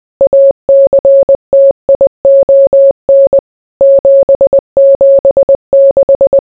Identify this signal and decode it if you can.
Morse code, heard as a keyed tone, about 20 wpm, 550 Hz. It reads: ACTION 776